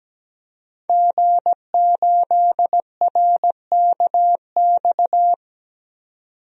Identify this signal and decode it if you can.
Morse code, heard as a keyed tone, about 17 wpm, 700 Hz. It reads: G8RKX
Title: DXCC-100 Call Signs - 5 Characters - 17wpm